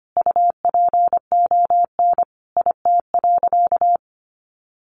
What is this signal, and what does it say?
Morse code, heard as a keyed tone, about 25 wpm, 700 Hz. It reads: UPON IT.
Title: Sentences from Top 1500 Words 25wpm